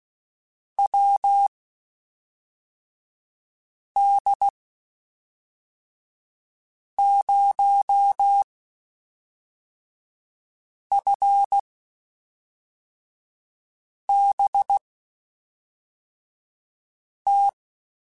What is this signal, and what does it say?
Morse code, heard as a keyed tone, about 16 wpm, spaced as for 3 wpm, 800 Hz. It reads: WD0FBT
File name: QSOs / QSOs-mp3s / WD0FBT 1@5